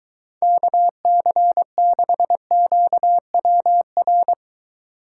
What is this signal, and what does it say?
Morse code, heard as a keyed tone, about 23 wpm, 700 Hz. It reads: KC6QWR